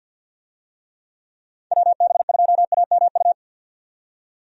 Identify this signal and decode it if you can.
Morse code, heard as a keyed tone, about 51 wpm, 700 Hz. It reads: WB2AMU
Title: USA Call Signs - 6 Characters - 50wpm